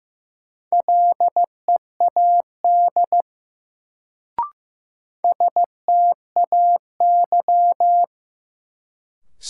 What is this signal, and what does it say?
Morse code, heard as a keyed tone, about 15 wpm, 700 Hz. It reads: LEAD STAY
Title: Sentences from Top 500 Words - New Word Drill - 15wpm